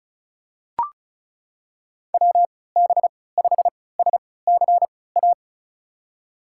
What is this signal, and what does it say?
Morse code, heard as a keyed tone, about 36 wpm, spaced as for 20 wpm, 700 Hz. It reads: WB5SCA
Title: USA Call Signs - 6 Characters - 3x Character Spacing - 35wpm